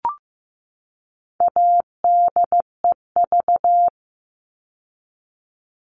Morse code, heard as a keyed tone, about 15 wpm, 700 Hz. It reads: ADEV
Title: All Cut Numbers - 4-Digit 15wpm